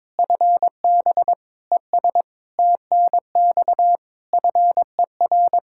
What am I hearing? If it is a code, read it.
Morse code, heard as a keyed tone, about 21 wpm, 700 Hz. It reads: FB ES TNX FER
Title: K1USN SST QSO Practice 22wpm